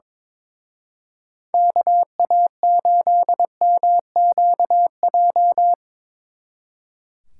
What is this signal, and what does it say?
Morse code, heard as a keyed tone, about 25 wpm, 700 Hz. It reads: KA8MQJ
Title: Speed Racing - USA 2x3 Call Signs 22wpm